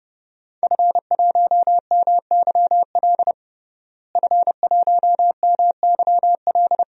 Morse code, heard as a keyed tone, about 30 wpm, 700 Hz. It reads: F1MYL F1MYL
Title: DXCC-100 Call Signs - 5 Characters - Repeated 3x - 30wpm